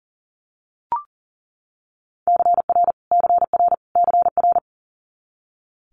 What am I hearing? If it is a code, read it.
Morse code, heard as a keyed tone, about 40 wpm, 690 Hz. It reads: CR CR CR
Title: Multi-letter Phonemes Repeated 3x 40wpm